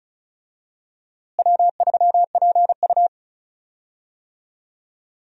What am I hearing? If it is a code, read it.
Morse code, heard as a keyed tone, about 36 wpm, 700 Hz. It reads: W3PU
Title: USA Call Signs - 4 Characters - Repeated 2x - 35wpm